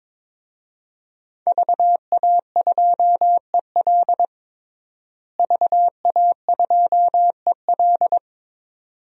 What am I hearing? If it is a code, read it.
Morse code, heard as a keyed tone, about 22 wpm, 700 Hz. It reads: VA2EL VA2EL
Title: DXCC-100 Call Signs - 5 Characters - Repeated 2x - 22wpm